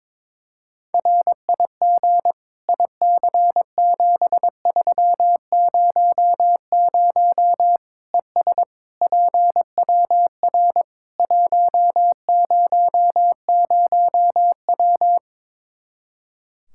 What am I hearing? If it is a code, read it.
Morse code, heard as a keyed tone, about 22 wpm, 700 Hz. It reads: RIG IC7300 ES PWR 100W
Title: Practice QSO - Exchange 2 - Info - 22wpm